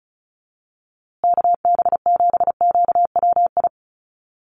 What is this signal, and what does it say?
Morse code, heard as a keyed tone, about 34 wpm, 700 Hz. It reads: KB7QWI
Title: USA Call Signs Repeated 3x 35wpm